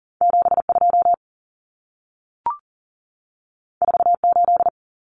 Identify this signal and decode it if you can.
Morse code, heard as a keyed tone, about 40 wpm, 700 Hz. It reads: 72 48
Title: Numbers - 2-Digit 40wpm